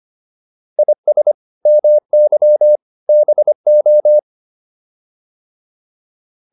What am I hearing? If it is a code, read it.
Morse code, heard as a keyed tone, about 25 wpm, 600 Hz. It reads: IS MY BO